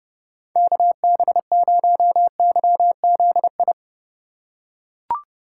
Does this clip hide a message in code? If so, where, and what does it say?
Yes: Morse code from 0.6 to 3.7 s, heard as a keyed tone, about 28 wpm, 700 Hz. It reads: KB0YZI